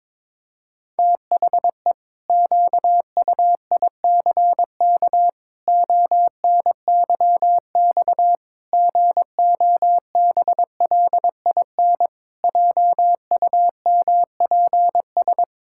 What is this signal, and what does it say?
Morse code, heard as a keyed tone, about 22 wpm, 700 Hz. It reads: THE QUICK ONYX GOBLIN JUMPS
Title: Pangram Challenge v2 - 22wpm